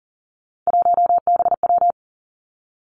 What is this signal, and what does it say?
Morse code, heard as a keyed tone, about 38 wpm, 700 Hz. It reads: 1BW
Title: Three Challenging Letter-Number Combinations 40wpm